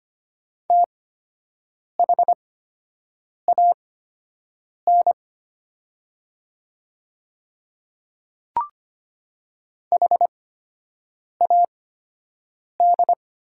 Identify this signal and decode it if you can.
Morse code, heard as a keyed tone, about 25 wpm, spaced as for 7 wpm, 700 Hz. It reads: THAN HAD
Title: 8x Character Spacing - Top 100 Words - 25wpm